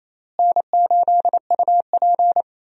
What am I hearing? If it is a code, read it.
Morse code, heard as a keyed tone, about 28 wpm, 700 Hz. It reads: N8UP
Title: USA Call Signs - 4 Characters - Repeated 3x - 28wpm